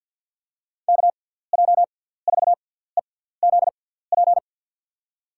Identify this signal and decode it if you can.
Morse code, heard as a keyed tone, about 49 wpm, spaced as for 17 wpm, 700 Hz. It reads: KJ4EZP
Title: USA Call Signs - 6 Characters - 6x Character Spacing - 50wpm